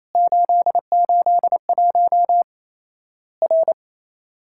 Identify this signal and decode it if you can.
Morse code, heard as a keyed tone, about 28 wpm, 700 Hz. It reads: 881 R
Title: Worked All Europe DX Contest - QTC Exchange - 28wpm